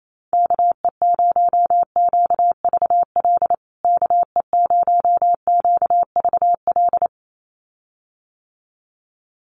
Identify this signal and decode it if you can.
Morse code, heard as a keyed tone, about 28 wpm, 700 Hz. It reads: KE0QVL KE0QVL
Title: USA Call Signs Repeated 2x 28wpm